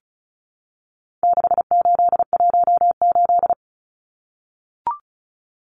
Morse code, heard as a keyed tone, about 34 wpm, 700 Hz. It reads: 6818
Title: Numbers - 4-Digit 35wpm